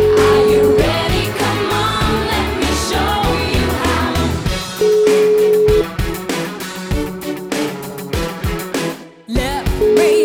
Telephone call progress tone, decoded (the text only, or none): ringback tone